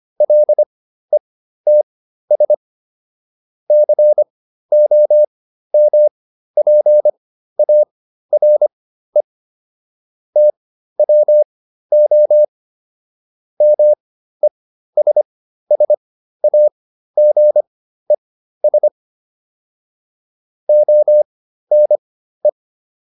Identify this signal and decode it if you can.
Morse code, heard as a keyed tone, about 25 wpm, spaced as for 13 wpm, 600 Hz. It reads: LETS COMPARE TWO MESSAGES ONE